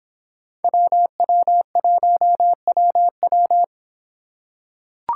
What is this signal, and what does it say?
Morse code, heard as a keyed tone, about 26 wpm, 700 Hz. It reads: WW1WW